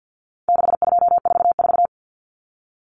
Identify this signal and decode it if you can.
Morse code, heard as a keyed tone, about 50 wpm, 700 Hz. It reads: 6244
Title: Numbers - 4-Digit 50wpm